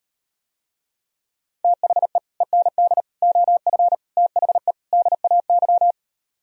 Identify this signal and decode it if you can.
Morse code, heard as a keyed tone, about 39 wpm, 700 Hz. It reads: THE END OF THE DAY